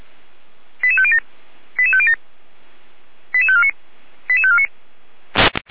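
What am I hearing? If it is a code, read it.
Five-tone selective call (ZVEI), heard as quick runs of five tones, tones of about 70 ms. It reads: 88507 88507 88549 88549
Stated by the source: ZVEI five-tone selective call received on SDR.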